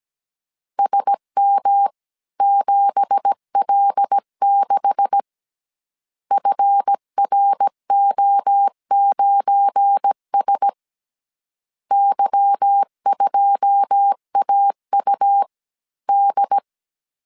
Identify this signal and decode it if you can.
Morse code, heard as a keyed tone, about 17 wpm, 800 Hz. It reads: SM7L6 FRO9S Y2AUD
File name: CW / DET / 1 9 1500C S C 16wpm 2909